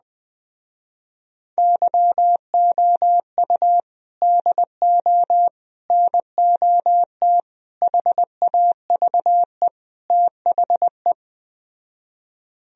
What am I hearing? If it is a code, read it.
Morse code, heard as a keyed tone, about 20 wpm, 700 Hz. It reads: YOU DO NOT HAVE THE